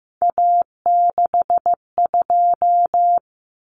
Morse code, heard as a keyed tone, about 15 wpm, 700 Hz. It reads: A62
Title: Common Cut Numbers - 3-Digit 15wpm